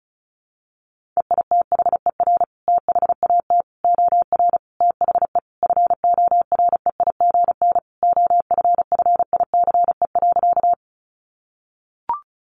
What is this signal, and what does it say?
Morse code, heard as a keyed tone, about 34 wpm, 700 Hz. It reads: EITHER THAT OR THE FOREIGN OFFICE.
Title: Sentences from Top 1400 Words 35wpm